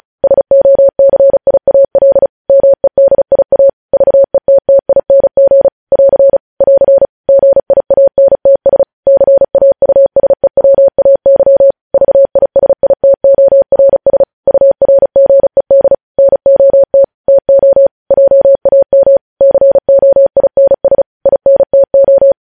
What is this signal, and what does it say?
Morse code, heard as a keyed tone, about 36 wpm, 550 Hz. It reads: SOCIAL MEDIA VETTING <AR> <AR> GIANTS CAUSEWAY VISITORS URGED NOT TO JAM COINS INTO
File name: podcast / News headlines-1x-35x35